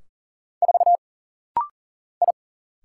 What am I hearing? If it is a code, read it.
Morse code, heard as a keyed tone, about 40 wpm, 700 Hz. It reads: 4 I
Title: Single Letter-Number - Rapid-Fire 40wpm